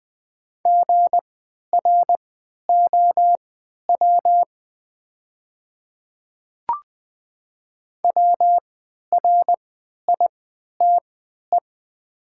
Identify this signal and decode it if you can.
Morse code, heard as a keyed tone, about 20 wpm, spaced as for 11 wpm, 700 Hz. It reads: GROW WRITE
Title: 3x Character Spacing - Top 500 Words - 20wpm